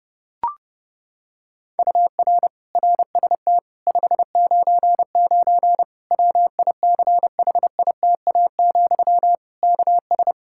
Morse code, heard as a keyed tone, about 30 wpm, 700 Hz. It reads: UR RST 599 WICHITA, KS
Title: Practice QSO - Exchange 1 - Beginning - 30wpm